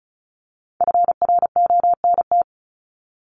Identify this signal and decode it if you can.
Morse code, heard as a keyed tone, about 34 wpm, 700 Hz. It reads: FRONT
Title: Top 500 Words Repeated 35wpm